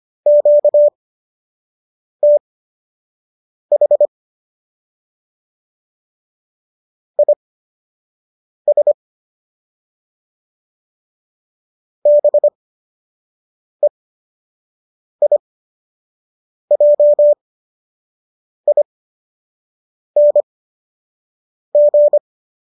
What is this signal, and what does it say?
Morse code, heard as a keyed tone, about 25 wpm, spaced as for 6 wpm, 590 Hz. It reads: QTH IS BEIJING
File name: qso002 06